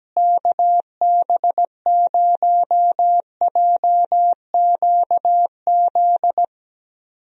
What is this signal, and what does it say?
Morse code, heard as a keyed tone, about 17 wpm, 700 Hz. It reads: KB0JQZ
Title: USA Call Signs - 6 Characters - Repeated 3x - 17wpm